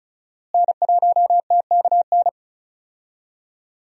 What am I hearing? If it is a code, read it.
Morse code, heard as a keyed tone, about 36 wpm, 700 Hz. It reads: N1TKN
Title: USA Call Signs - 5 Characters - Repeated 2x - 35wpm